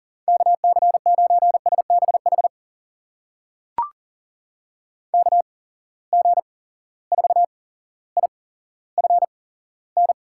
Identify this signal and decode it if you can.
Morse code, heard as a keyed tone, about 40 wpm, spaced as for 17 wpm, 700 Hz. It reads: KC9SBH KG4IFN
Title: USA Call Signs - 8x Character Spacing - 40wpm